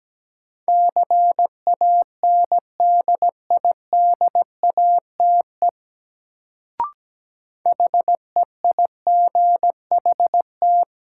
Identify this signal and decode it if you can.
Morse code, heard as a keyed tone, about 17 wpm, 700 Hz. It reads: CANDIDATE HEIGHT
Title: Sentences from Top 1600 Words - New Word Drill Alt - 17wpm